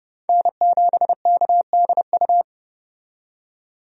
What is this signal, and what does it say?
Morse code, heard as a keyed tone, about 30 wpm, 700 Hz. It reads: N7KDU
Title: USA Call Signs - 5 Characters - 30wpm